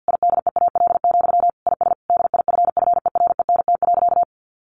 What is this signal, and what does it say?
Morse code, heard as a keyed tone, about 50 wpm, 700 Hz. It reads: IDEAL, IS DIFFERENT.
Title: Sentences from Top 1400 Words 50wpm